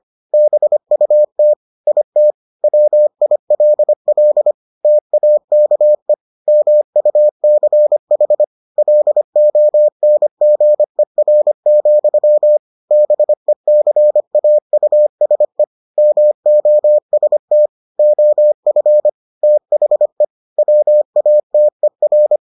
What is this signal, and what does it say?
Morse code, heard as a keyed tone, about 25 wpm, 600 Hz. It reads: BUT IT WILL TAKE MUCH LONGER, BECAUSE MOST OF THE WATER